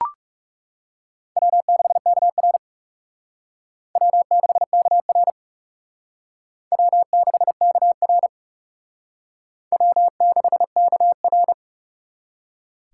Morse code, heard as a keyed tone, about 36 wpm, 700 Hz. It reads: W6KR W6KR W6KR W6KR
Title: Speed Racing - USA 1x2 Call Signs 30wpm